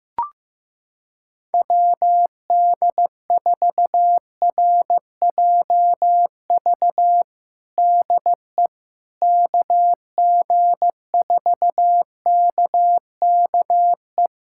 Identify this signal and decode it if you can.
Morse code, heard as a keyed tone, about 15 wpm, 700 Hz. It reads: WD4RJV DE KG4KKE
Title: USA Callsign DE Callsign - 6 Characters - 15wpm